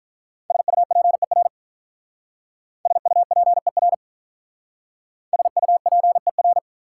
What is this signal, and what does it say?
Morse code, heard as a keyed tone, about 44 wpm, 700 Hz. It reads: SUPER SUPER SUPER